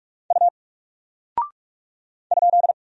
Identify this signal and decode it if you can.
Morse code, heard as a keyed tone, about 44 wpm, 700 Hz. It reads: U ?
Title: Single Character (Letter:Number:Punctuation:Prosign) 45wpm